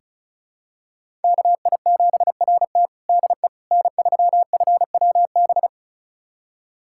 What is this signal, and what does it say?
Morse code, heard as a keyed tone, about 34 wpm, 700 Hz. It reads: KI7RT DE N3FWB
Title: USA Callsign DE Callsign - 5 Characters - 35wpm